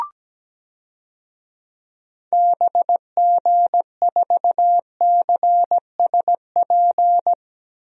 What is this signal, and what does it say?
Morse code, heard as a keyed tone, about 17 wpm, 700 Hz. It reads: BG4CSP
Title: DXCC-100 Call Signs - 6 Characters - Repeated 3x - 17wpm